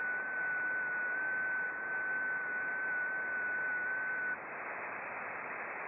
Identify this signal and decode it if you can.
FT8.
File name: contests / sept-vhf-k5tr- -0300